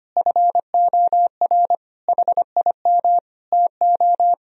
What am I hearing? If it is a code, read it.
Morse code, heard as a keyed tone, about 25 wpm, 700 Hz. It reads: FOR HIM TO
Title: Sentences from Top 1000 Words - Review Part 2 - 25wpm